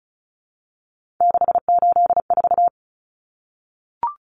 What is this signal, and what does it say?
Morse code, heard as a keyed tone, about 34 wpm, 700 Hz. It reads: 684
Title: Numbers - 3-Digit 35wpm